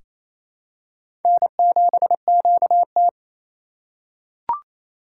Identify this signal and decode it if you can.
Morse code, heard as a keyed tone, about 28 wpm, 700 Hz. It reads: N7QT
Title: K1USN SST QSO Practice - 28wpm